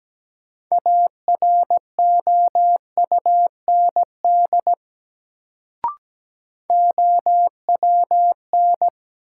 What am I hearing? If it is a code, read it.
Morse code, heard as a keyed tone, about 17 wpm, 700 Hz. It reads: AROUND OWN
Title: Sets of 2 Words - New Word Drill - 17wpm